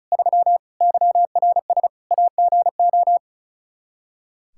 Morse code, heard as a keyed tone, about 36 wpm, 700 Hz. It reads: 3 YRS AGO
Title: Practice QSO - Exchange 3 - More Info - 35wpm